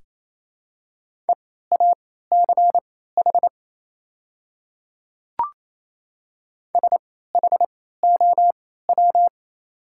Morse code, heard as a keyed tone, about 28 wpm, spaced as for 16 wpm, 700 Hz. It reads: EACH SHOW
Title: 3x Character Spacing - Top 200 Words - 28wpm